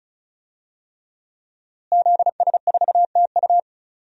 Morse code, heard as a keyed tone, about 36 wpm, 700 Hz. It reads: ZS4TU
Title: DXCC-100 Call Signs - 5 Characters - Repeated 3x - 35wpm